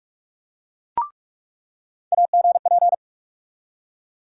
Morse code, heard as a keyed tone, about 42 wpm, 700 Hz. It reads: AGP
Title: USA Call Sign Postfixes - 3 Letters 45wpm